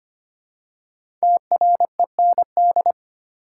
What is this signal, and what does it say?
Morse code, heard as a keyed tone, about 25 wpm, 700 Hz. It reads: TREND
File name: 5 Letter Words - First 3 Letters Given - 25wpm